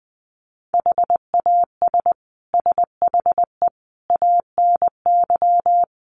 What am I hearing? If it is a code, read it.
Morse code, heard as a keyed tone, about 20 wpm, 700 Hz. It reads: HAS SHE ANY
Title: Sentences from Top 400 Words 20wpm